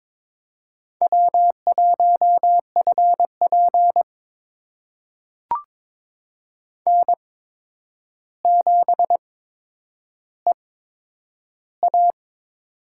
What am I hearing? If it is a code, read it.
Morse code, heard as a keyed tone, about 22 wpm, spaced as for 9 wpm, 700 Hz. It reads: W1FP N7EA
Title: USA Call Signs - 4 Characters - 8x Character Spacing - 22wpm